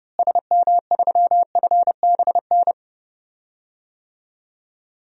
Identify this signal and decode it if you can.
Morse code, heard as a keyed tone, about 30 wpm, 700 Hz. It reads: SM3FBN
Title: DXCC-100 Call Signs - 6 Characters - Repeated 2x - 30wpm